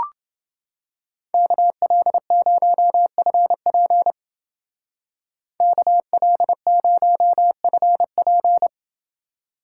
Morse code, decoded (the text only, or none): KL0FP KL0FP